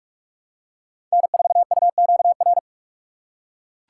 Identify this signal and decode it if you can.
Morse code, heard as a keyed tone, about 42 wpm, 700 Hz. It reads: N4UQR
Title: USA Call Signs - 5 Characters - 3x Character Spacing - 45wpm